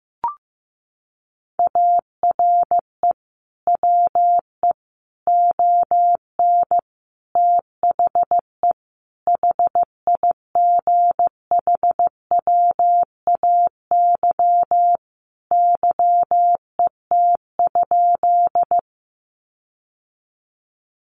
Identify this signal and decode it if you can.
Morse code, heard as a keyed tone, about 15 wpm, 700 Hz. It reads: ARE WE ON THE HIGHWAY YET?
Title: Sentences from Top 1400 Words 15wpm